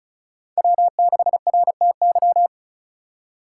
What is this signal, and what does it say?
Morse code, heard as a keyed tone, about 36 wpm, 700 Hz. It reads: W6RTY
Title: USA Call Signs - 2x Character Spacing - 35wpm